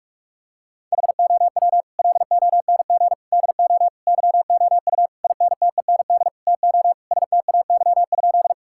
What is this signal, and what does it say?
Morse code, heard as a keyed tone, about 46 wpm, 700 Hz. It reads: HOW LONG DO YOU INTEND TO STAY?
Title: Sentences from Top 1600 Words 45wpm